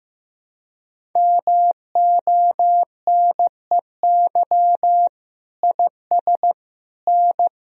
Morse code, heard as a keyed tone, about 15 wpm, 700 Hz. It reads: MONEY IS N
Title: Sentences from Top 500 Words and Abbreviations 15wpm